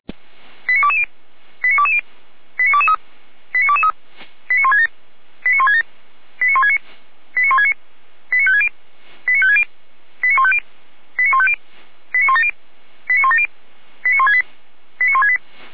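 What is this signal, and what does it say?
Five-tone selective call (ZVEI), heard as quick runs of five tones, tones of about 80 ms. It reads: 89220 89220 89223 89223 89167 89167 89169 89169 89560 89560 89150 89150 89170 89170 89168 89168
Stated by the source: ZVEI five-tone selective call received by SDR.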